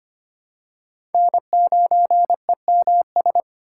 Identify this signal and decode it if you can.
Morse code, heard as a keyed tone, about 25 wpm, 700 Hz. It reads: N9EMS
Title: USA Call Signs - 5 Characters - 3x Character Spacing - 25wpm